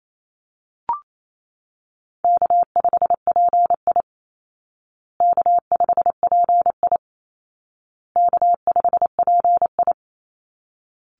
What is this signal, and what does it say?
Morse code, heard as a keyed tone, about 28 wpm, 700 Hz. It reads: K5PI K5PI K5PI
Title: Easy Call Signs Repeated 3x - 28wpm